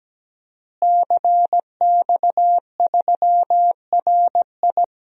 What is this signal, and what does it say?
Morse code, heard as a keyed tone, about 17 wpm, 700 Hz. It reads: CX3RI
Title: DXCC-100 Call Signs - 5 Characters - Repeated 3x - 17wpm